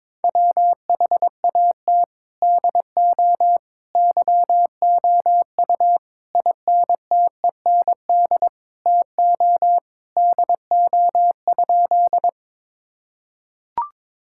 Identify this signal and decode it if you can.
Morse code, heard as a keyed tone, about 22 wpm, 700 Hz. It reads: WHAT DO YOU INTEND TO DO?
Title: Sentences from Top 1600 Words 22wpm